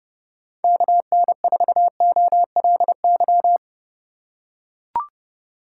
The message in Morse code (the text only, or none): KN4OLY